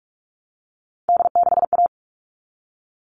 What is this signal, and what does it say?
Morse code, heard as a keyed tone, about 44 wpm, 700 Hz. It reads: D6A
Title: All Cut Numbers - 3-Digit 45wpm